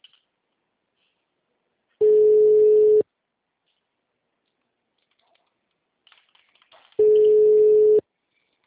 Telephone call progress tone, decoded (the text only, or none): ringback tone